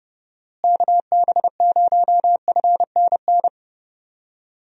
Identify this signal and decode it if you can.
Morse code, heard as a keyed tone, about 32 wpm, 700 Hz. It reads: KB0FNN
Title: USA Call Signs - 6 Characters - Repeated 2x - 30wpm